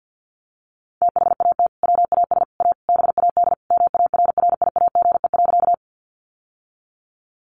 Morse code, heard as a keyed tone, about 50 wpm, 700 Hz. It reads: THAT WAS A BAD MARRIAGE.
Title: Sentences from Top 800 Words 50wpm